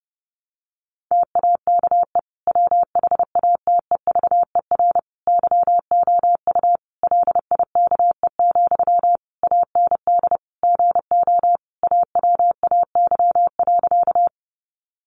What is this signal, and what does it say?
Morse code, heard as a keyed tone, about 30 wpm, 700 Hz. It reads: TAKE WHATEVER YOU LIKE, AND GO AWAY.
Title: Sentences from Top 1100 Words 30wpm